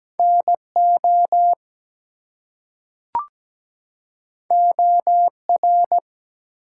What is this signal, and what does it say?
Morse code, heard as a keyed tone, about 17 wpm, 700 Hz. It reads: NO OR
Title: Two Letter Words - 17wpm